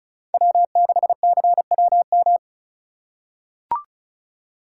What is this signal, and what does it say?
Morse code, heard as a keyed tone, about 34 wpm, 700 Hz. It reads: W6CWM
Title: K1USN SST QSO Practice - 35wpm